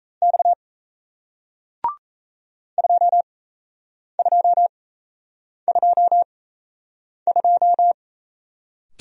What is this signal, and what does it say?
Morse code, heard as a keyed tone, about 38 wpm, spaced as for 8 wpm, 700 Hz. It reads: X 2222